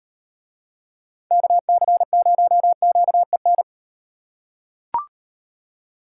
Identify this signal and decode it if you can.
Morse code, heard as a keyed tone, about 38 wpm, 700 Hz. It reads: KC0QEN